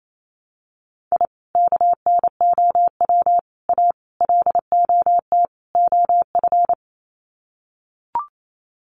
Morse code, heard as a keyed tone, about 28 wpm, 700 Hz. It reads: I KNOW A LOT OF
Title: Sets of 5 Words Encore 28wpm